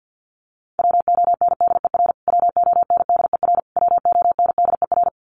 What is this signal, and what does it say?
Morse code, heard as a keyed tone, about 50 wpm, 700 Hz. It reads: WONDER WONDER WONDER
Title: Top 500 Words Repeated 50wpm